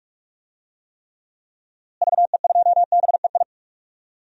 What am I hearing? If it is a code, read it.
Morse code, heard as a keyed tone, about 42 wpm, 700 Hz. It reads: VE2BEI
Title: DXCC-100 Call Signs - 6 Characters - Repeated 3x - 45wpm